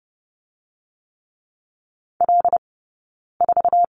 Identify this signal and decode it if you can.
Morse code, heard as a keyed tone, about 32 wpm, 700 Hz. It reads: L 4
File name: Lesson 18d - Letter m - Two Characters 30wpm